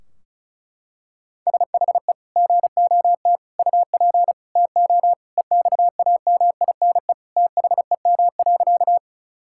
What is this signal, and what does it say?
Morse code, heard as a keyed tone, about 36 wpm, 700 Hz. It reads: SHE GOT UP TO EXAMINE THEM.
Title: Sentences from Top 1600 Words 35wpm